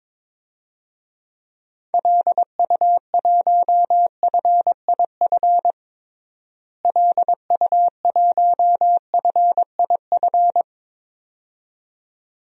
Morse code, heard as a keyed tone, about 21 wpm, 700 Hz. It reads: LU1FIF LU1FIF
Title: DXCC-100 Call Signs - 6 Characters - Repeated 2x - 22wpm